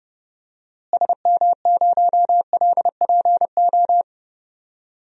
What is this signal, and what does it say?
Morse code, heard as a keyed tone, about 30 wpm, 700 Hz. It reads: SM0LPO
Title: DXCC-100 Call Signs - 6 Characters - Repeated 2x - 30wpm